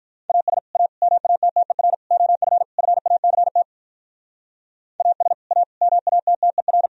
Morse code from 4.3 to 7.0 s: AS A MATTER

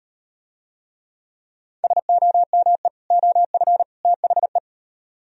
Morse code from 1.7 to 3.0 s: SOME

